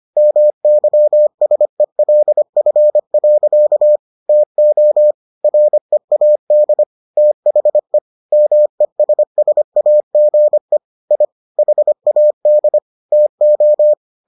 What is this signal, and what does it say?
Morse code, heard as a keyed tone, about 25 wpm, 600 Hz. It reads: MYSELF. TO READ THE MESSAGE I HAD TO